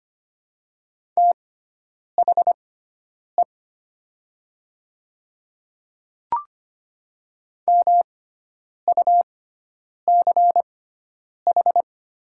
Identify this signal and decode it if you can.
Morse code, heard as a keyed tone, about 25 wpm, spaced as for 9 wpm, 700 Hz. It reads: THE MUCH
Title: 6x Character Spacing - Top 300 Words - 25wpm